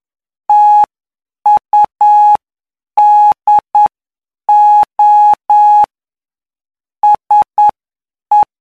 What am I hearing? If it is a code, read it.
Morse code, heard as a keyed tone, about 9 wpm, spaced as for 7 wpm, 840 Hz. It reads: TUDO SE
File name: cw / py2lcd / 17-PY2LCD-Li��o 26